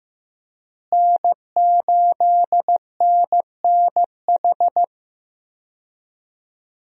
Morse code, heard as a keyed tone, about 16 wpm, 700 Hz. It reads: N8NNH
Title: USA Call Signs - 5 Characters - Repeated 2x - 15wpm